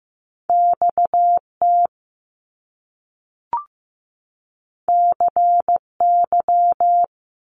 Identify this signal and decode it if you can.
Morse code, heard as a keyed tone, about 15 wpm, 700 Hz. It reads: XT CY
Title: Top Two Letter Combinations in Words 15wpm